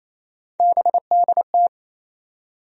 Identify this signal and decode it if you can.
Morse code, heard as a keyed tone, about 27 wpm, 700 Hz. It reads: BDT
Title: USA Call Sign Postfixes - 3 Letters 28wpm